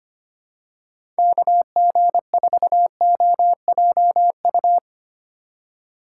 Morse code, heard as a keyed tone, about 25 wpm, 700 Hz. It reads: KG4OJU
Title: USA Call Signs - 6 Characters - 3x Character Spacing - 25wpm